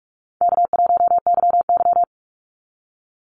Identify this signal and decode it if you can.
Morse code, heard as a keyed tone, about 44 wpm, 700 Hz. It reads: K1YY
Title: Call Signs - 45wpm